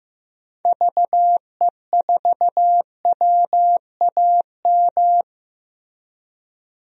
Morse code, heard as a keyed tone, about 15 wpm, 700 Hz. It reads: VE4WAM
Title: DXCC-100 Call Signs - 6 Characters - Repeated 3x - 15wpm